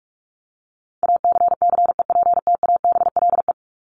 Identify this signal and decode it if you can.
Morse code, heard as a keyed tone, about 44 wpm, 700 Hz. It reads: ACCEPTABLE
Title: Top 1501-1750 Words 45wpm